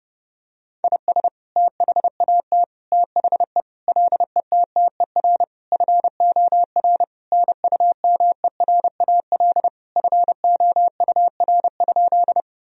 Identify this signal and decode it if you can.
Morse code, heard as a keyed tone, about 32 wpm, 700 Hz. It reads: IS THAT THE LETTER FOR NUMERAL FOUR?
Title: Sentences from Top 1000 Words - Review Part 2 - 30wpm